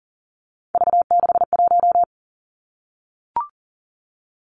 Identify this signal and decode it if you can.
Morse code, heard as a keyed tone, about 42 wpm, 700 Hz. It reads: V61
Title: Three Challenging Letter-Number Combinations 40wpm